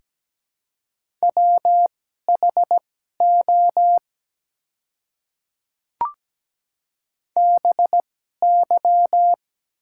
Morse code, heard as a keyed tone, about 17 wpm, spaced as for 12 wpm, 700 Hz. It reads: WHO BY
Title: 2x Character Spacing - Top 100 Words - 17wpm